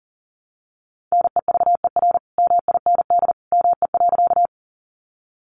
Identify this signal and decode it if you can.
Morse code, heard as a keyed tone, about 40 wpm, 700 Hz. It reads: NEVER MIND ME.
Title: Sentences from Top 500 Words 40wpm